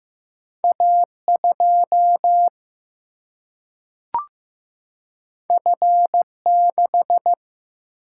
Morse code, heard as a keyed tone, about 15 wpm, 700 Hz. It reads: A2 F6
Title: Single-Letter Single-Number 15wpm